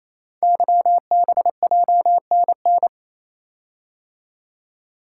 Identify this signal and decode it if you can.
Morse code, heard as a keyed tone, about 29 wpm, 700 Hz. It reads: YBJNN
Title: Code Groups - 5 Letters - 28wpm